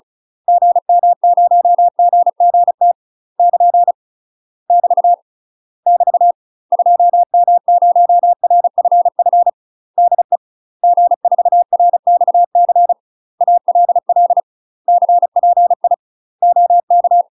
Morse code, heard as a keyed tone, about 34 wpm, 700 Hz. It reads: GM0GGT <KN> <BT> <BT> 2M0RFF DE G4RXC ALL CPI OK
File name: UK MASTER QSOs TRAINING/35WPM